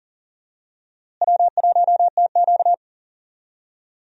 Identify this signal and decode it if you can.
Morse code, heard as a keyed tone, about 40 wpm, 700 Hz. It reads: W1TQ
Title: USA Call Signs - 4 Characters - Repeated 2x - 40wpm